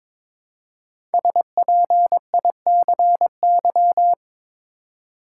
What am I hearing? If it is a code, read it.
Morse code, heard as a keyed tone, about 22 wpm, 700 Hz. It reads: SPICY
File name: 5 Letter Words - First 3 Letters Given - 22wpm